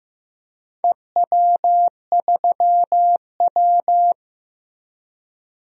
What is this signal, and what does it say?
Morse code, heard as a keyed tone, about 15 wpm, 700 Hz. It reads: EW3W
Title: DXCC-100 Call Signs - 4 Characters - Repeated 2x - 15wpm